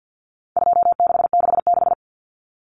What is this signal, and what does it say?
Morse code, heard as a keyed tone, about 50 wpm, 700 Hz. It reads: 2666
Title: Common Cut Numbers - 4-Digit 50wpm